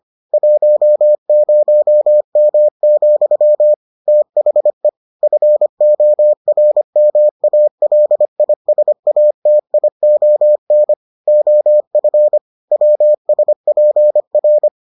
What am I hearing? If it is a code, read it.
Morse code, heard as a keyed tone, about 25 wpm, 600 Hz. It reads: 10M, THE FORMALISATION OF WSPR